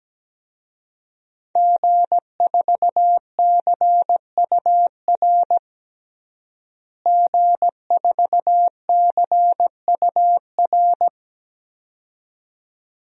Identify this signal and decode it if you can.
Morse code, heard as a keyed tone, about 17 wpm, 700 Hz. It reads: G4CUR G4CUR
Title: DXCC-100 Call Signs - 5 Characters - Repeated 2x - 17wpm